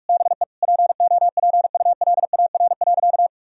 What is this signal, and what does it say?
Morse code, heard as a keyed tone, about 46 wpm, 700 Hz. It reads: BE POPULAR.
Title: Sentences from Top 1000 Words - Review Part 4 - 45wpm